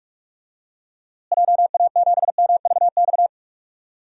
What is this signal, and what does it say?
Morse code, heard as a keyed tone, about 46 wpm, 700 Hz. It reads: JA7MVX
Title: DXCC-100 Call Signs - 6 Characters - Repeated 3x - 45wpm